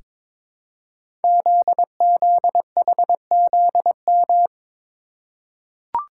Morse code, heard as a keyed tone, about 23 wpm, 700 Hz. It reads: ZZHZM
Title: Code Groups - 5 Letters - 22wpm